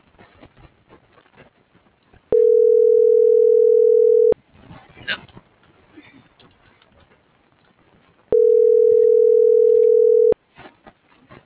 Ringback tone (North America), 440 + 480 Hz.